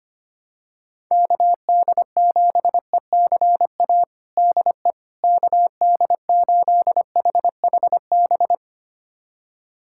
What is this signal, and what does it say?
Morse code, heard as a keyed tone, about 25 wpm, 700 Hz. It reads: KD7ECA DE KD8HHB
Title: USA Callsign DE Callsign - 6 Characters - 25wpm